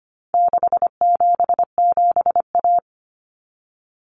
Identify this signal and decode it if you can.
Morse code, heard as a keyed tone, about 25 wpm, 700 Hz. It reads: 677A
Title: Common Cut Numbers - 4-Digit 25wpm